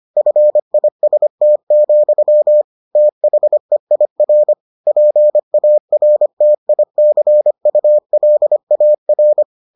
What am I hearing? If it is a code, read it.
Morse code, heard as a keyed tone, about 25 wpm, 600 Hz. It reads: FIST, THEIR PARTICULAR